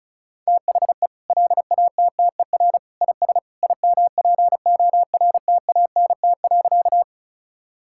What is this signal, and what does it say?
Morse code, heard as a keyed tone, about 34 wpm, 700 Hz. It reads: THE LATTER IS IMPORTANT.
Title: Sentences from Top 1600 Words 35wpm